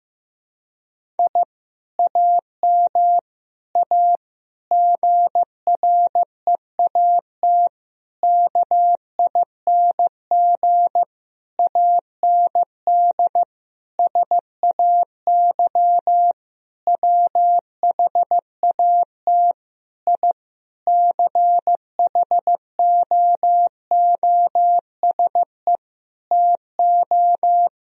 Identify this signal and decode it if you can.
Morse code, heard as a keyed tone, about 15 wpm, 700 Hz. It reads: I AM A GREAT KING AND SAY WHAT I CHOOSE TO
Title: Sentences from Top 1000 Words - Review Part 4 - 15wpm